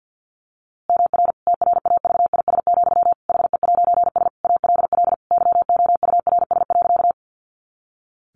Morse code, heard as a keyed tone, about 50 wpm, 700 Hz. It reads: MR TRAVIS, HE'S ALL YOURS.
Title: A Sound Of Thunder 50wpm